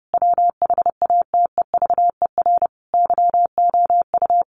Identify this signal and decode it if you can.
Morse code, heard as a keyed tone, about 30 wpm, 700 Hz. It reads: WHATEVER YOU
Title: Sentences from Top 1500 Words 30wpm